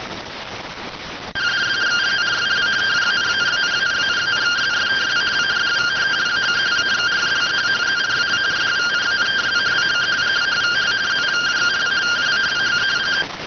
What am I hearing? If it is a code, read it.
Two-tone teleprinter signal, about 170 Hz shift.